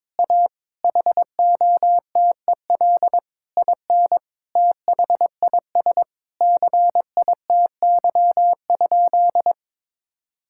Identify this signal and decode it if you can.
Morse code, heard as a keyed tone, about 22 wpm, 700 Hz. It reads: A HOTEL IN THIS CITY?
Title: Sentences from Top 1600 Words 22wpm